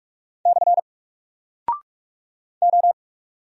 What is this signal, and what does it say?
Morse code, heard as a keyed tone, about 44 wpm, 700 Hz. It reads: / O
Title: Single Character (Letter:Number:Punctuation:Prosign) Alt 45wpm